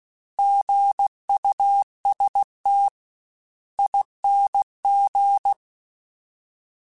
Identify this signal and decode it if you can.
Morse code, heard as a keyed tone, about 16 wpm, spaced as for 13 wpm, 800 Hz. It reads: GUSTING
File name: Alphabetical / Alphabetical mp3s / Gs @15